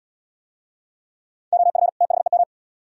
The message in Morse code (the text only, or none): OF THEM